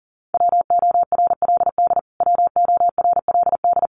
Morse code, heard as a keyed tone, about 37 wpm, 700 Hz. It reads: WORLD WORLD
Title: Top 500 Words Repeated 40wpm